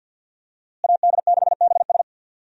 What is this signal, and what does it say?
Morse code, heard as a keyed tone, about 47 wpm, 700 Hz. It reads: AD6BS
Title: USA Call Signs - 5 Characters - Repeated 3x - 50wpm